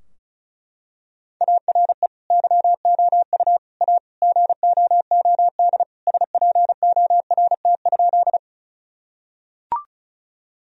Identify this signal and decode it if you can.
Morse code, heard as a keyed tone, about 34 wpm, 700 Hz. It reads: ARE YOU A GOOD SPORT?
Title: Sentences from Top 1000 Words - Review Part 4 - 35wpm